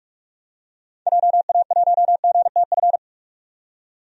Morse code, heard as a keyed tone, about 46 wpm, 700 Hz. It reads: JA1GTF
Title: DXCC-100 Call Signs - 6 Characters - 45wpm